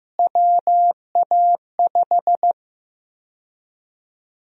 Morse code, heard as a keyed tone, about 16 wpm, 700 Hz. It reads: WA5
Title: USA Call Sign Prefixes - Repeated 2x - 15wpm